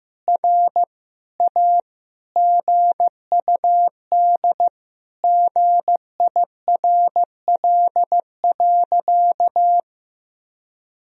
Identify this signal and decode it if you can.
Morse code, heard as a keyed tone, about 15 wpm, 700 Hz. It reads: R A GUD GIRL.
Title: Sentences from Top 400 Words and Abbreviations 15wpm